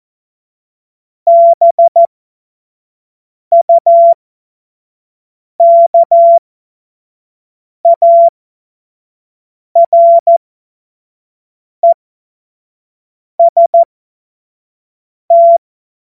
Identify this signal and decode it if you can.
Morse code, heard as a keyed tone, about 14 wpm, spaced as for 5 wpm, 680 Hz. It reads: BUKAREST
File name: Morsekurs / Witze / Witz 07. Februar 2023 25 F70